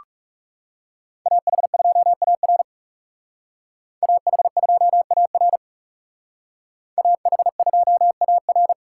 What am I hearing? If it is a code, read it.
Morse code, heard as a keyed tone, about 38 wpm, 700 Hz. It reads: AH2AR AH2AR AH2AR